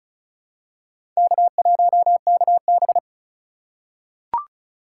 Morse code, heard as a keyed tone, about 36 wpm, 700 Hz. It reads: K1KB
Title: USA Call Signs - 4 Characters - 3x Character Spacing - 35wpm